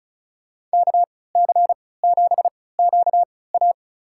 Morse code, heard as a keyed tone, about 36 wpm, spaced as for 20 wpm, 700 Hz. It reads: KC7QA
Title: USA Call Signs - 5 Characters - 3x Character Spacing - 35wpm